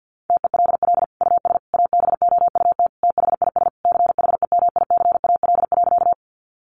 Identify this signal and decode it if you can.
Morse code, heard as a keyed tone, about 49 wpm, 700 Hz. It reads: TELL US ABOUT THIS CHEMICAL.
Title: Sentences from Top 1200 Words 50wpm